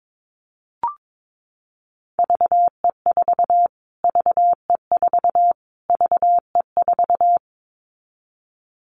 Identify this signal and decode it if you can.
Morse code, heard as a keyed tone, about 22 wpm, 700 Hz. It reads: VE4 VE4 VE4
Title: Callsign Prefixes for Top 25 DXCC - Repeated 3x - 22wpm